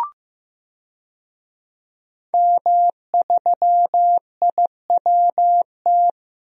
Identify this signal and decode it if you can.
Morse code, heard as a keyed tone, about 15 wpm, 700 Hz. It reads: M3IWT
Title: DXCC-100 Call Signs - 5 Characters - Repeated 3x - 15wpm